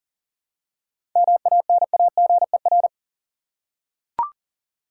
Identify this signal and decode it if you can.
Morse code, heard as a keyed tone, about 40 wpm, 700 Hz. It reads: MANAGER
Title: Seven Letter Words - 40wpm